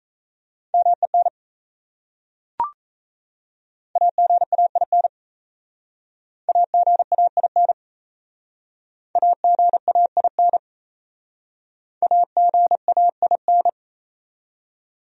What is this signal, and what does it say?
Morse code, heard as a keyed tone, about 37 wpm, 700 Hz. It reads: MEN AGAIN AGAIN AGAIN AGAIN